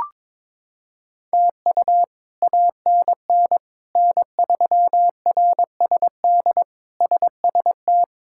Morse code, heard as a keyed tone, about 22 wpm, 700 Hz. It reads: TU ANN N3RSD SST
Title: K1USN SST QSO Practice - 22wpm